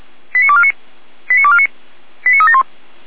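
Five-tone selective call (ZVEI), heard as quick runs of five tones, tones of about 70 ms. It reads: 89249 89249 89371
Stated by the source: ZVEI five-tone selective call received with an SDR.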